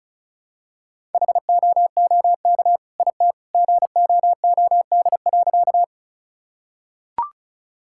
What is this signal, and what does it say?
Morse code, heard as a keyed tone, about 36 wpm, 700 Hz. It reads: HOOK IT GOOD.
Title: Sentences from Top 1600 Words 35wpm